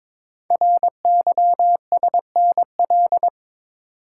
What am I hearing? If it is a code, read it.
Morse code, heard as a keyed tone, about 22 wpm, 700 Hz. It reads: RYSNL
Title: Code Groups - 5 Letters - 22wpm